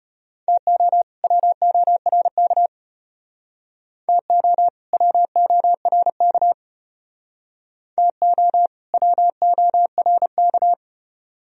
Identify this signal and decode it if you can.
Morse code, heard as a keyed tone, about 33 wpm, 700 Hz. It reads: TO WORK TO WORK TO WORK